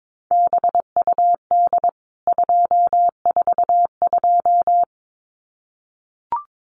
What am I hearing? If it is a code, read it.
Morse code, heard as a keyed tone, about 22 wpm, 700 Hz. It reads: BUD 242
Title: CWT Exchange 22wpm